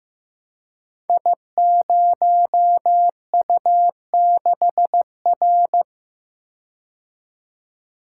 Morse code, heard as a keyed tone, about 15 wpm, 700 Hz. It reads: I0U6R
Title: Code Groups - 5 Letter:Numbers - 15wpm